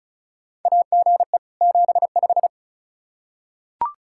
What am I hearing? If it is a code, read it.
Morse code, heard as a keyed tone, about 36 wpm, 700 Hz. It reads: AGE 75
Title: K1USN SST QSO Practice 35wpm